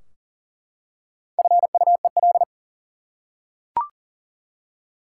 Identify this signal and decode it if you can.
Morse code, heard as a keyed tone, about 37 wpm, 700 Hz. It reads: FUEL
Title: Four Letter Words - 40wpm